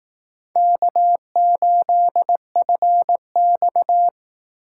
Morse code, heard as a keyed tone, about 19 wpm, 700 Hz. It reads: K8FX
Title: Speed Racing - USA 1x2 Call Signs 15wpm